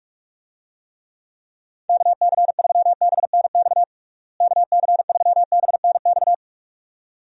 Morse code, heard as a keyed tone, about 46 wpm, 700 Hz. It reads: KC3BNX KC3BNX
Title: USA Call Signs - 6 Characters - Repeated 2x - 45wpm